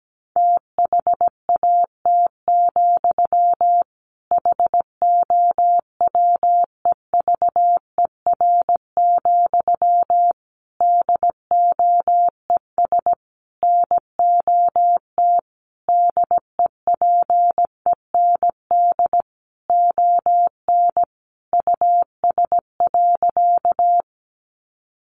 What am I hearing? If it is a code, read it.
Morse code, heard as a keyed tone, about 17 wpm, 700 Hz. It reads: THAT, HOWEVER, DOES NOT DEPEND ON US.
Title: Sentences from Top 1300 Words 17wpm